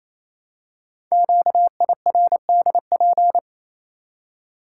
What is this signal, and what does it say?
Morse code, heard as a keyed tone, about 27 wpm, 700 Hz. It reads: QIRDP
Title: Code Groups - 5 Letter:Numbers - 28wpm